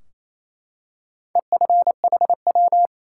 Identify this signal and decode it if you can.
Morse code, heard as a keyed tone, about 27 wpm, 700 Hz. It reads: EFHW
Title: Top 180 QSO Elements 28wpm